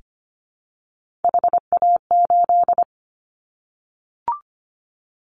Morse code, heard as a keyed tone, about 25 wpm, 700 Hz. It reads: HA8
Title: Callsign Prefixes for Top 25 DXCC - 25wpm